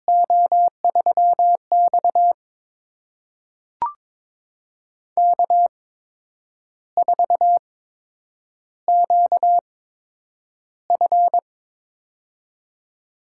Morse code, heard as a keyed tone, about 22 wpm, spaced as for 9 wpm, 700 Hz. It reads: O3X K4QF
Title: USA Call Signs - 4 Characters - 8x Character Spacing - 22wpm